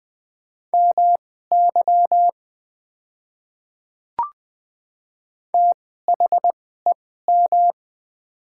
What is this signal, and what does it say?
Morse code, heard as a keyed tone, about 20 wpm, spaced as for 14 wpm, 700 Hz. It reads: MY THEM
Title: 2x Character Spacing - Top 100 Words - 20wpm